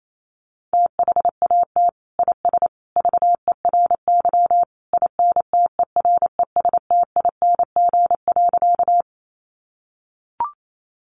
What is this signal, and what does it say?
Morse code, heard as a keyed tone, about 28 wpm, 700 Hz. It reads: THAT IS VERY INTERESTING.
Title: Sentences from Top 1000 Words 28wpm